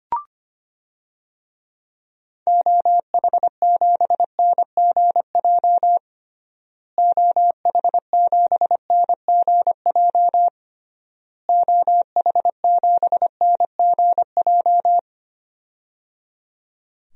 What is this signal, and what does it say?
Morse code, heard as a keyed tone, about 25 wpm, 700 Hz. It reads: OH7NGJ OH7NGJ OH7NGJ
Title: DXCC-100 Call Signs - 6 Characters - Repeated 3x - 25wpm